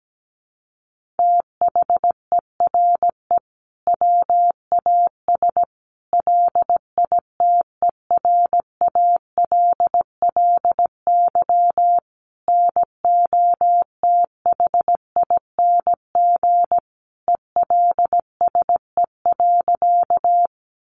Morse code, heard as a keyed tone, about 17 wpm, 700 Hz. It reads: THERE WAS LITERALLY NOTHING ELSE.
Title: Sentences from Top 1400 Words 17wpm